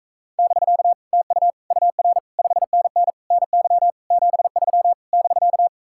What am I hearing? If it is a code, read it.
Morse code, heard as a keyed tone, about 45 wpm, 700 Hz. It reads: <BK> TU UR 5NN NY 73 <BK>